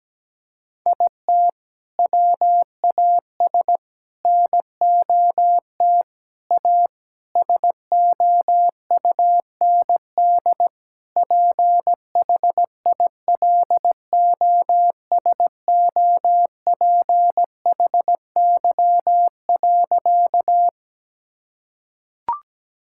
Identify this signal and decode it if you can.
Morse code, heard as a keyed tone, about 17 wpm, 700 Hz. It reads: IT WAS NOT A SOUND PHILOSOPHY.
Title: Sentences from Top 1000 Words - Review Part 6 - 17wpm